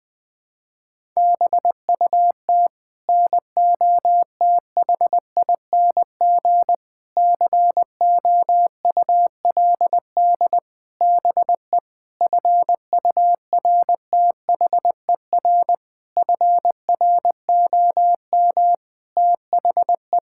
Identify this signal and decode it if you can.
Morse code, heard as a keyed tone, about 20 wpm, 700 Hz. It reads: BUT NOTHING COULD BE FURTHER FROM THE
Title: Sets of 8 Words - 20wpm